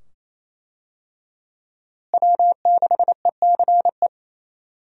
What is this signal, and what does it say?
Morse code, heard as a keyed tone, about 28 wpm, 700 Hz. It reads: W6ECE
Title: USA Call Signs - 5 Characters - Repeated 2x - 28wpm